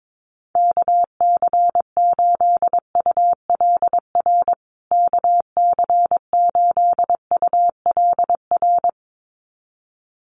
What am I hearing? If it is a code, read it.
Morse code, heard as a keyed tone, about 22 wpm, 700 Hz. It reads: KC8ULR KC8ULR
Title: USA Call Signs Repeated 2x 22wpm